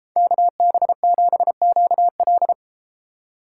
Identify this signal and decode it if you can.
Morse code, heard as a keyed tone, about 30 wpm, 700 Hz. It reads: KB7QL